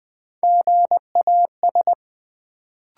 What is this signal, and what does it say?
Morse code, heard as a keyed tone, about 20 wpm, 700 Hz. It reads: GAS
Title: Sentences from Top 1100 Words - New Word Drill - 20wpm